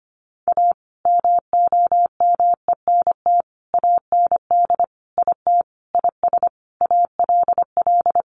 Morse code, heard as a keyed tone, about 25 wpm, 700 Hz. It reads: A MOMENT AND IT IS ALL
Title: Sentences from Top 1000 Words 25wpm